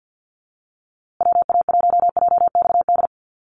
Morse code, heard as a keyed tone, about 50 wpm, 700 Hz. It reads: WA1JXD
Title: USA Call Signs Repeated 2x 50wpm